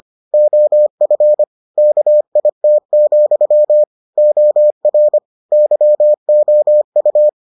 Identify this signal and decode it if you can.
Morse code, heard as a keyed tone, about 25 wpm, 600 Hz. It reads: OF KIT, OR YOU